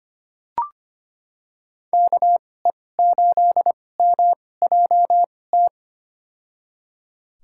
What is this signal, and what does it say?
Morse code, heard as a keyed tone, about 25 wpm, spaced as for 18 wpm, 700 Hz. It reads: KE8MJT
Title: USA Call Signs - 6 Characters - 2x Character Spacing - 25wpm